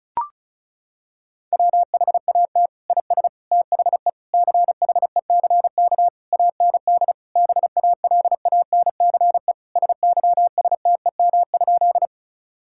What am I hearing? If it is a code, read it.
Morse code, heard as a keyed tone, about 36 wpm, 700 Hz. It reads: WHAT IS THE CHECK AND BALANCE SYSTEM?
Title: Sentences from Top 1000 Words - Review Part 6 - 35wpm